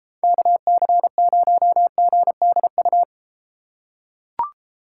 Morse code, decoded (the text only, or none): KC0GDU